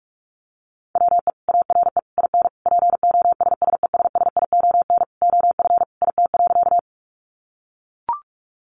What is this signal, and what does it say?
Morse code, heard as a keyed tone, about 44 wpm, 700 Hz. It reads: WE ARE IN POSSESSION OF IT.
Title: Sentences from Top 1400 Words 45wpm